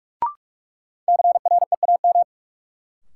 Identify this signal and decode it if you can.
Morse code, heard as a keyed tone, about 46 wpm, 700 Hz. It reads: CREAM
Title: Sentences from Top 1500 Words - New Word Drill - 45wpm